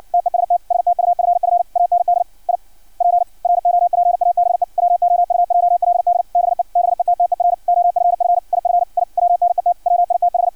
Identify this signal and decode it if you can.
Morse code, heard as a keyed tone, about 56 wpm, 700 Hz. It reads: TEST STUFF AND I PROPABE WOUJLD BE BETTER OFF IF I WNET WITH
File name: archived / CRSnet